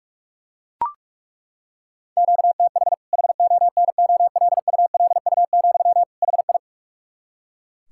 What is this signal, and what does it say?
Morse code, heard as a keyed tone, about 41 wpm, 700 Hz. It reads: QTH HONOLULU, HI
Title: K1USN SST QSO Practice 45wpm